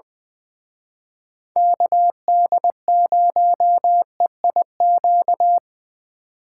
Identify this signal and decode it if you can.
Morse code, heard as a keyed tone, about 25 wpm, 700 Hz. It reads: KD0EIQ
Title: Speed Racing - USA 2x3 Call Signs 20wpm